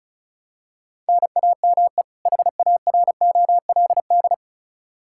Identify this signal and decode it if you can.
Morse code, heard as a keyed tone, about 36 wpm, 700 Hz. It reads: NAME HAROLD
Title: Practice QSO - Exchange 1 - Beginning - 35wpm